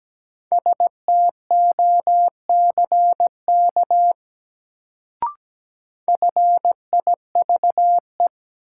Morse code, heard as a keyed tone, about 17 wpm, 700 Hz. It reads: STOCK FIVE
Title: Sets of 5 Words - New Word Drill - 17wpm